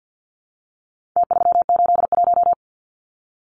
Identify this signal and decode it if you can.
Morse code, heard as a keyed tone, about 50 wpm, 700 Hz. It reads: T381
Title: Numbers - Contest Style - 4-Digit 50wpm